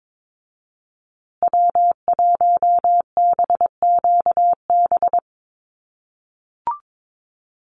Morse code, heard as a keyed tone, about 22 wpm, 700 Hz. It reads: W1BQB
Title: USA Call Signs Repeated 2x 22wpm